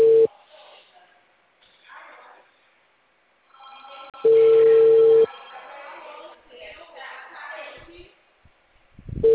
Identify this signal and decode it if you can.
Ringback tone (China, 450 Hz).
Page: 154.53.55.99 Dracena Call